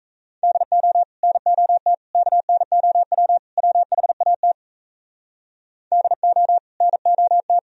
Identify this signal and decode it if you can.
Morse code, heard as a keyed tone, about 37 wpm, 700 Hz. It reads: DO NOT KNOW WHAT DO NOT